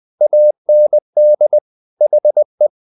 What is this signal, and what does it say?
Morse code, heard as a keyed tone, about 20 wpm, 600 Hz. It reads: AND HE
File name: the old man and the sea part 2 20wpm